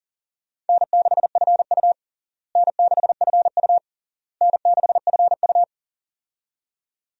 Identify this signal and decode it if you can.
Morse code, heard as a keyed tone, about 40 wpm, 700 Hz. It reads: N6FU N6FU N6FU
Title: USA Call Signs - 4 Characters - Repeated 3x - 40wpm